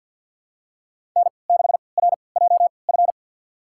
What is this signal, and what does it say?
Morse code, heard as a keyed tone, about 47 wpm, spaced as for 28 wpm, 700 Hz. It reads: N6RJF
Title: USA Call Signs - 5 Characters - 3x Character Spacing - 50wpm